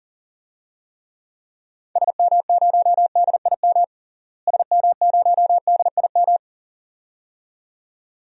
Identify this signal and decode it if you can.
Morse code, heard as a keyed tone, about 37 wpm, 700 Hz. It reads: SM0DIM SM0DIM
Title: DXCC-100 Call Signs - 6 Characters - Repeated 2x - 40wpm